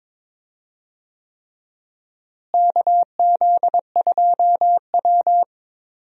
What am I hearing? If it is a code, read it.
Morse code, heard as a keyed tone, about 22 wpm, 700 Hz. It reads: KZ2W
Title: USA Call Signs - 4 Characters - Repeated 3x - 22wpm